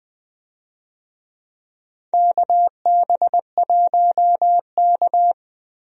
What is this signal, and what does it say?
Morse code, heard as a keyed tone, about 20 wpm, 700 Hz. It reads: KB1K
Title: USA Call Signs - 4 Characters - Repeated 3x - 20wpm